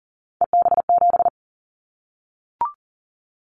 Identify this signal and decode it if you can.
Morse code, heard as a keyed tone, about 40 wpm, 700 Hz. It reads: EB7
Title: Callsign Prefixes for Top 25 DXCC - 40wpm